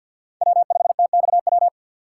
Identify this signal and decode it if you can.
Morse code, heard as a keyed tone, about 51 wpm, 700 Hz. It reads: W5TXW
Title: USA Call Signs - 5 Characters - 3x Character Spacing - 50wpm